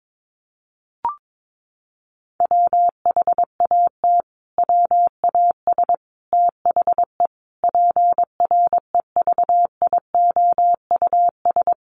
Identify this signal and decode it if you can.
Morse code, heard as a keyed tone, about 22 wpm, 700 Hz. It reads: WHAT WAS THE PREVIOUS
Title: Sentences from Top 1500 Words 22wpm